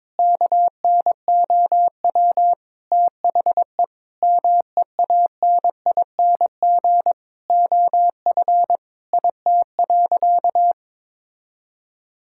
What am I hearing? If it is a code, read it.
Morse code, heard as a keyed tone, about 22 wpm, 700 Hz. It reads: KNOW THE MEANING OF IT.
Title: Sentences from Top 1000 Words - Review Part 6 - 22wpm